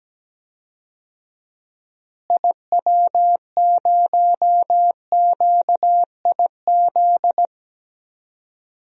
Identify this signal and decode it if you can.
Morse code, heard as a keyed tone, about 17 wpm, 700 Hz. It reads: IW0QIZ
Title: DXCC-100 Call Signs - 6 Characters - Repeated 3x - 17wpm